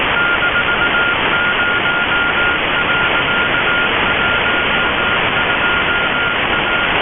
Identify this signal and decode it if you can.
Morse code, heard as a keyed tone, about 18 wpm, 1410 Hz, with radio noise. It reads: -0.12A
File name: HSU-SAT1 437,280 CW 1x beacon, 2122utc